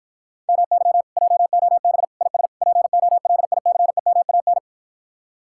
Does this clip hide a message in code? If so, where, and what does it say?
Yes: Morse code from 0.5 to 4.6 s, heard as a keyed tone, about 54 wpm, 700 Hz. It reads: MY JOB IS POLICEMAN